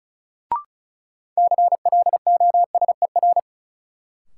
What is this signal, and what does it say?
Morse code, heard as a keyed tone, about 34 wpm, 700 Hz. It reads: CLOSER
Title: Sets of 5 Words Encore - New Word Drill - 35wpm